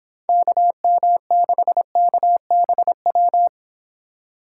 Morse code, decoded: KM6KBW

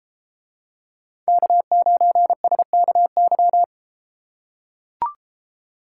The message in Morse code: K9SKY

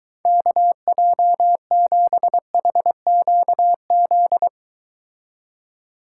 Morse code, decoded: KJ7HQZ

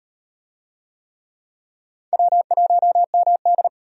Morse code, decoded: W1MD

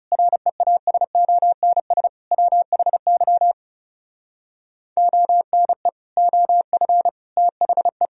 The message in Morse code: REASONS WHY ONE OF THE

